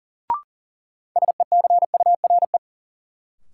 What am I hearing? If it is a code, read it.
Morse code, heard as a keyed tone, about 40 wpm, 700 Hz. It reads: SECURE
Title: Sentences from Top 1200 Words - New Word Drill - 40wpm